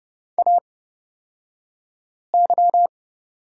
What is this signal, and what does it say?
Morse code, heard as a keyed tone, about 30 wpm, spaced as for 5 wpm, 700 Hz. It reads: AY
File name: Lesson 37d - Letter x - Two Characters 30wpm